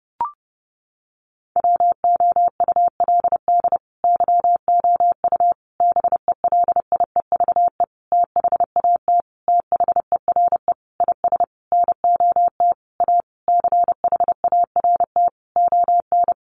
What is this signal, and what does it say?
Morse code, heard as a keyed tone, about 30 wpm, 700 Hz. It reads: WOULD YOU BELIEVE THAT THERE IS NOT A CHART ON
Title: Sentences from Top 1300 Words 30wpm